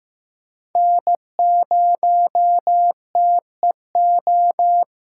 Morse code, heard as a keyed tone, about 15 wpm, 700 Hz. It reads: N0TEO
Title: USA Callsign DE Callsign - 5 Characters - 15wpm